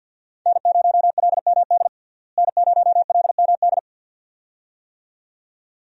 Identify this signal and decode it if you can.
Morse code, heard as a keyed tone, about 52 wpm, 700 Hz. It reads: N0LMD N0LMD
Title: USA Call Signs - 5 Characters - Repeated 2x - 50wpm